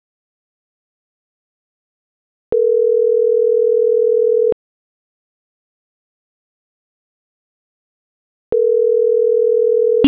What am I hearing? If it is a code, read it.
Ringback tone (North America), 440 + 480 Hz.
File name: CENTRAL MAERSK / out sandovalmsk